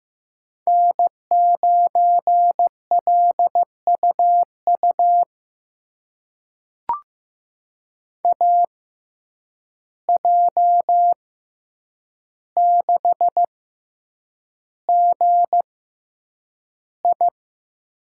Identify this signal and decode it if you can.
Morse code, heard as a keyed tone, about 15 wpm, spaced as for 8 wpm, 700 Hz. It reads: N9LUU AJ6GI
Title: USA Call Signs - 5 Characters - 6x Character Spacing - 15wpm